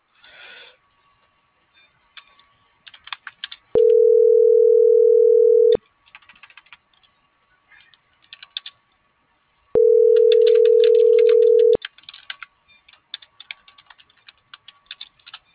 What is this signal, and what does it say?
Ringback tone (North America), 440 + 480 Hz.